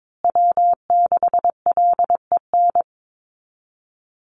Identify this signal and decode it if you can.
Morse code, heard as a keyed tone, about 22 wpm, 700 Hz. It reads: W6LEN
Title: Call Signs Repeated 3x - 22wpm